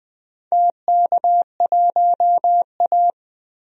Morse code, heard as a keyed tone, about 21 wpm, 700 Hz. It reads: TK1A
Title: DXCC-100 Call Signs - 4 Characters - 20wpm